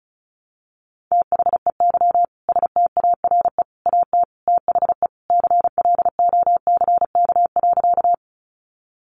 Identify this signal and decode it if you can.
Morse code, heard as a keyed tone, about 34 wpm, 700 Hz. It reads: THEY STARE AT THE CLOCK.
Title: Easy Phrases 35wpm